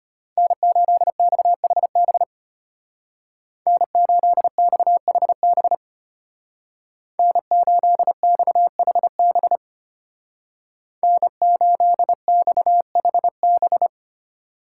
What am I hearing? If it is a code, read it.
Morse code, heard as a keyed tone, about 33 wpm, 700 Hz. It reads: N8XHB N8XHB N8XHB N8XHB